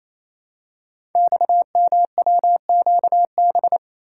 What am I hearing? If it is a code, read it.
Morse code, heard as a keyed tone, about 27 wpm, 700 Hz. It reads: XMWQB
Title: Code Groups - 5 Letter:Numbers - 28wpm